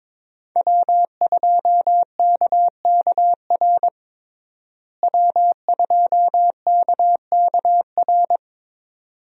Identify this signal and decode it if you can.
Morse code, heard as a keyed tone, about 22 wpm, 700 Hz. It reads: W2KKR W2KKR
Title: USA Call Signs - 5 Characters - Repeated 2x - 22wpm